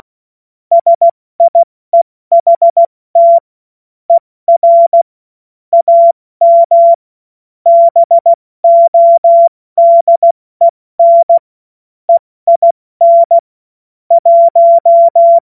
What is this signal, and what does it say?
Morse code, heard as a keyed tone, about 16 wpm, 680 Hz. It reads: SIEHT ER AM BODEN EIN 1
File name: Morsekurs / Witze / Witz 17. August 2021, 70 F80